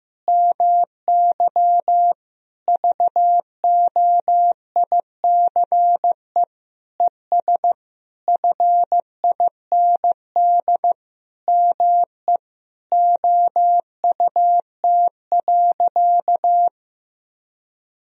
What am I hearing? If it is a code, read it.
Morse code, heard as a keyed tone, about 15 wpm, 700 Hz. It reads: MY VOICE ES FIND ME OUT.
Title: Sentences from Top 500 Words and Abbreviations 15wpm